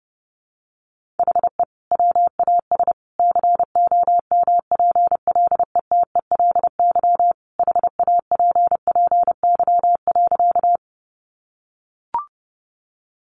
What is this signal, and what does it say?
Morse code, heard as a keyed tone, about 28 wpm, 700 Hz. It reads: HE WAS COMPLETELY HAPPY.
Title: Sentences from Top 1100 Words 30wpm